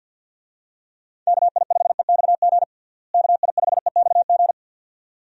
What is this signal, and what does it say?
Morse code, heard as a keyed tone, about 49 wpm, 700 Hz. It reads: KI5EXG KI5EXG
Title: USA Call Signs - 6 Characters - Repeated 2x - 50wpm